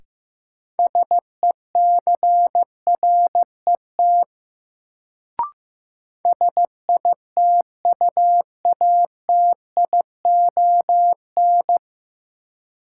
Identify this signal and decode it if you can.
Morse code, heard as a keyed tone, about 16 wpm, 700 Hz. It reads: SECRET SITUATION
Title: Sentences from Top 1200 Words - New Word Drill - 15wpm